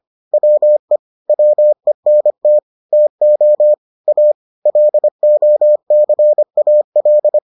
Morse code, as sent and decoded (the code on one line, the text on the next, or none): .-- . / .-- . -. - / - --- / .- / .-.. --- -.-. .- .-..
WE WENT TO A LOCAL